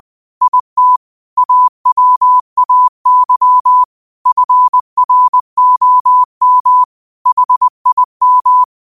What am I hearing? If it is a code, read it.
Morse code, heard as a keyed tone, about 20 wpm, 1000 Hz. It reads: IT AWAY FROM HIM
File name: qotdfiles / 20wpm